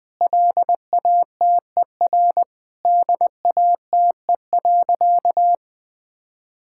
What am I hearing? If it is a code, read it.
Morse code, heard as a keyed tone, about 20 wpm, 700 Hz. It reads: LATER DATE.
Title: Sentences with 7-Word Phrases - Spoken Beginning - 20wpm